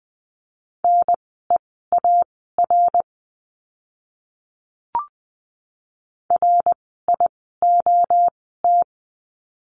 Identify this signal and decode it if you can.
Morse code, heard as a keyed tone, about 20 wpm, spaced as for 15 wpm, 700 Hz. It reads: NEAR RIOT
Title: 061 Lesson 11f - Letter h - Words 20wpm